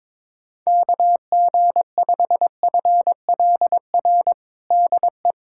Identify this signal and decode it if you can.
Morse code, heard as a keyed tone, about 21 wpm, 700 Hz. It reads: KG5FLR DE
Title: USA Callsign DE Callsign 22wpm